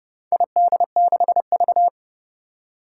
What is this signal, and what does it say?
Morse code, heard as a keyed tone, about 31 wpm, 700 Hz. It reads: ID6V
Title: Four Challenging Letter-Number Combinations 30wpm